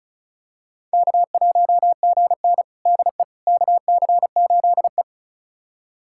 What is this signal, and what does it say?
Morse code, heard as a keyed tone, about 36 wpm, 700 Hz. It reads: K1GN DE KC8E
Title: USA Callsign DE Callsign - 4 Characters - 35wpm